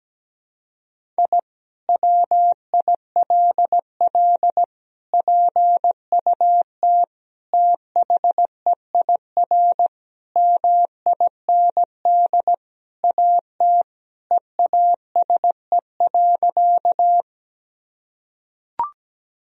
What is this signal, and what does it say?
Morse code, heard as a keyed tone, about 17 wpm, 700 Hz. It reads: I WILL PUT THEIR MIND AT EASE.
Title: Sentences from Top 1000 Words - Review Part 1 - 17wpm